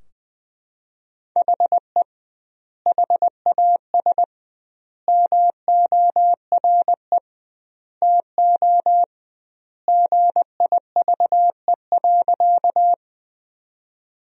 Morse code, heard as a keyed tone, about 20 wpm, 700 Hz. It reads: HE HAS MORE TO GIVE.
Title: 2x Word-Spacing - Sentences from Top 200 Words - 20wpm